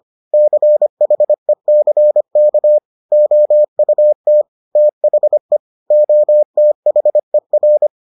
Morse code, heard as a keyed tone, about 25 wpm, 600 Hz. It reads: CHECK OUT THE OTHER